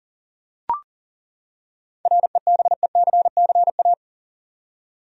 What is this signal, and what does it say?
Morse code, heard as a keyed tone, about 40 wpm, 700 Hz. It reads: REBECCA
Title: USA Common Names - 40wpm